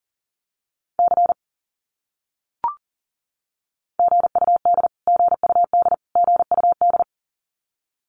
Morse code, heard as a keyed tone, about 40 wpm, 700 Hz. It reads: C GUD GUD GUD
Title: Abbreviations Repeated 3x - 40wpm